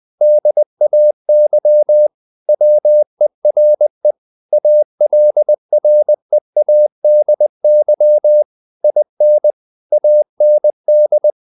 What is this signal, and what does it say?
Morse code, heard as a keyed tone, about 20 wpm, 600 Hz. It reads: DAY WERE ALREADY IN AND
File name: the old man and the sea / the old man and the sea part 1 20wpm